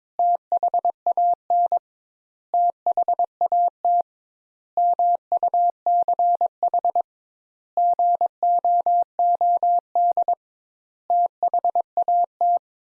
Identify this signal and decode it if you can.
Morse code, heard as a keyed tone, about 22 wpm, 700 Hz. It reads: THAN THAT MUCH GOOD THAT
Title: Sentences Top 200 Words - 22wpm